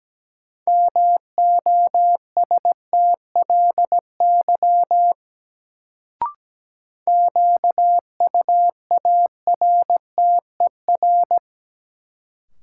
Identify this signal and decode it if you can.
Morse code, heard as a keyed tone, about 17 wpm, 700 Hz. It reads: MOSTLY QUARTER
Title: Sentences from Top 1300 Words - New Word Drill - 17wpm